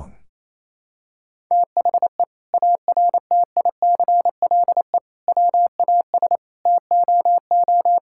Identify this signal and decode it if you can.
Morse code, heard as a keyed tone, about 28 wpm, 700 Hz. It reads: THE ARTICLE WAS TOO
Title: Sentences from Top 1000 Words - Review Part 5 - 28wpm